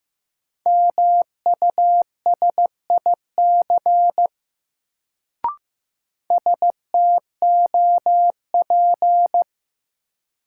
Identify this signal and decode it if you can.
Morse code, heard as a keyed tone, about 15 wpm, 700 Hz. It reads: MUSIC STOP
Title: Sentences from Top 300 Words - New Word Drill - 15wpm